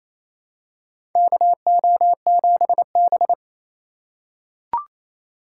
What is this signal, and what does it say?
Morse code, heard as a keyed tone, about 28 wpm, 690 Hz. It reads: KO7B
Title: USA Call Signs - 3x Character Spacing - 28wpm